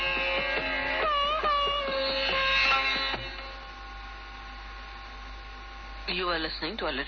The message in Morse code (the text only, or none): none